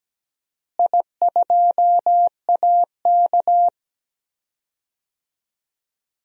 Morse code, heard as a keyed tone, about 17 wpm, 700 Hz. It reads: I2AK
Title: DXCC-100 Call Signs - 4 Characters - Repeated 3x - 17wpm